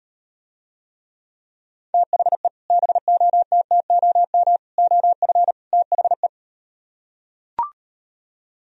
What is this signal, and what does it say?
Morse code, heard as a keyed tone, about 38 wpm, 700 Hz. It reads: THE BOTTOM OF THE